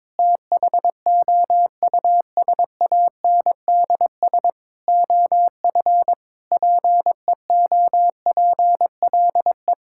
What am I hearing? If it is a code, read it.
Morse code, heard as a keyed tone, about 22 wpm, 700 Hz. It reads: THOUSANDS OF PEOPLE
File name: practice / 22wpm / Sentences with 4-Word Phrases - 22wpm